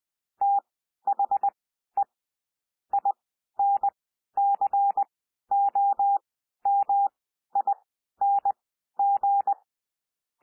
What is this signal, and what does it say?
Morse code, heard as a keyed tone, about 20 wpm, spaced as for 12 wpm, 800 Hz. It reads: THE INCOMING
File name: cwmp3 / P1 SOUND PASSAGE 20 wpm